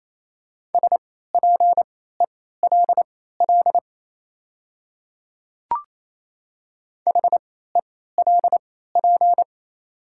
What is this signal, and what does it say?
Morse code, heard as a keyed tone, about 28 wpm, spaced as for 16 wpm, 700 Hz. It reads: SPELL HELP
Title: 3x Character Spacing - Top 200 Words - 28wpm